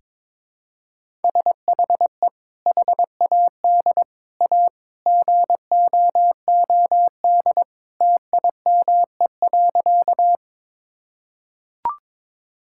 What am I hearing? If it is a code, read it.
Morse code, heard as a keyed tone, about 22 wpm, 700 Hz. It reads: SHE HAD A GOOD TIME.
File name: practice / 22wpm / Sentences with 2-Word Phrases - 22wpm